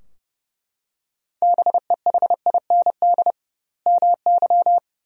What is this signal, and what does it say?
Morse code, heard as a keyed tone, about 32 wpm, 700 Hz. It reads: BEHIND MY
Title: 2x Word-Spacing - Sentences from Top 500 Words - 30wpm